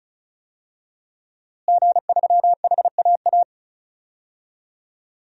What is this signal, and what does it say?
Morse code, heard as a keyed tone, about 36 wpm, 700 Hz. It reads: G3HAA
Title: DXCC-100 Call Signs - 5 Characters - Repeated 3x - 35wpm